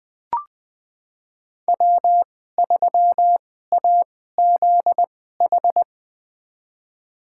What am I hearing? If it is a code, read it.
Morse code, heard as a keyed tone, about 20 wpm, spaced as for 14 wpm, 700 Hz. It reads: W3AZH
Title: USA Call Signs - 5 Characters - 2x Character Spacing - 20wpm